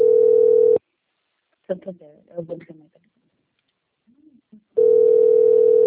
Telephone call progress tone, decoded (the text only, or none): ringback tone